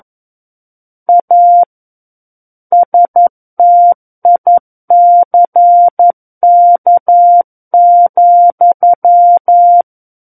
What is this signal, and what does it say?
Morse code, heard as a keyed tone, about 11 wpm, 700 Hz. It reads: A STICK,
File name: 11 WPM